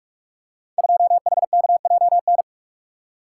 Morse code, heard as a keyed tone, about 43 wpm, 700 Hz. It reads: 2HKJN
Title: Code Groups - 5 Letter:Numbers - 45wpm